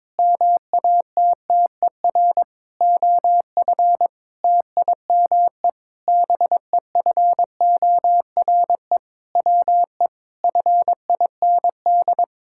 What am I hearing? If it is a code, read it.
Morse code, heard as a keyed tone, about 22 wpm, 700 Hz. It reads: MATTER OF TIME BEFORE WE FIND
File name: practice / 22wpm / Sentences with 8-Word Phrases - 22wpm